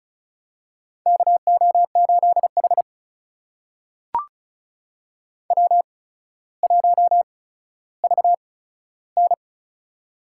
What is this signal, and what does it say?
Morse code, heard as a keyed tone, about 36 wpm, spaced as for 15 wpm, 700 Hz. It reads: KO8H W1VN
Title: USA Call Signs - 4 Characters - 8x Character Spacing - 35wpm